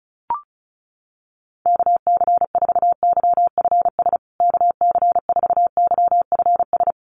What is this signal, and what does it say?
Morse code, heard as a keyed tone, about 34 wpm, 700 Hz. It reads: KC4YFS KC4YFS
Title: USA Call Signs Repeated 3x 35wpm